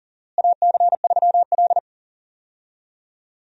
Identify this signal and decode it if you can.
Morse code, heard as a keyed tone, about 37 wpm, 700 Hz. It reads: AC3L
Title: USA Call Signs - 4 Characters - Repeated 3x - 40wpm